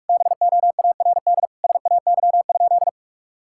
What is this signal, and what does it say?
Morse code, heard as a keyed tone, about 46 wpm, 700 Hz. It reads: BOARD SAY?
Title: Sentences from Top 1000 Words - Review Part 5 - 45wpm